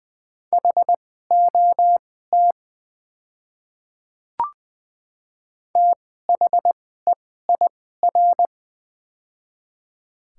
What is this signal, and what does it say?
Morse code, heard as a keyed tone, about 20 wpm, spaced as for 15 wpm, 690 Hz. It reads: HOT THEIR
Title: 2x Character Spacing - Top 300 Words - 20wpm